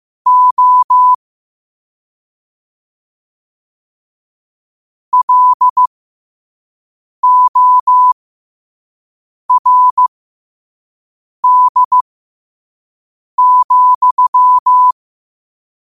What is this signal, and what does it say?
Morse code, heard as a keyed tone, about 15 wpm, spaced as for 5 wpm, 1000 Hz. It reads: O LORD,
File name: qotdfiles / 5wpm